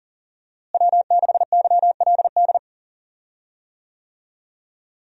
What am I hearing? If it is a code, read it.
Morse code, heard as a keyed tone, about 40 wpm, 700 Hz. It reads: W6YLD
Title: USA Call Signs - 5 Characters - Repeated 3x - 40wpm